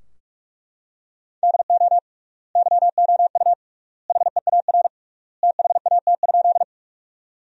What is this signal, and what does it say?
Morse code, heard as a keyed tone, about 42 wpm, 700 Hz. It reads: DO YOU HEAR THAT?
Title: 3x Word-Spacing - Sentences from Top 300 Words - 45wpm